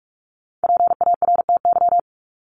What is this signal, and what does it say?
Morse code, heard as a keyed tone, about 44 wpm, 700 Hz. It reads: PARTY
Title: ES:S Suffix - Top 100 Words with Base Words 45wpm